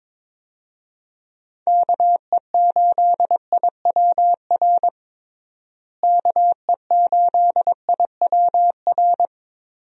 Morse code, heard as a keyed tone, about 22 wpm, 700 Hz. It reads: KE8IWR KE8IWR
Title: USA Call Signs - 6 Characters - Repeated 2x - 22wpm